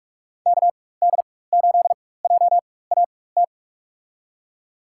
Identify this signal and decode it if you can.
Morse code, heard as a keyed tone, about 46 wpm, spaced as for 21 wpm, 700 Hz. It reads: KD8JAT
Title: USA Call Signs - 6 Characters - 4x Character Spacing - 45wpm